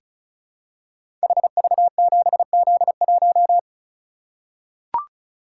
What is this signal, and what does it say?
Morse code, heard as a keyed tone, about 36 wpm, 690 Hz. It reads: HV7Z1
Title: Code Groups - 5 Letter:Numbers - 35wpm